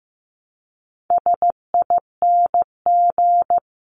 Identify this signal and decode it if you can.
Morse code, heard as a keyed tone, about 15 wpm, 700 Hz. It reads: SING
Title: Top 500 Words Repeated 15wpm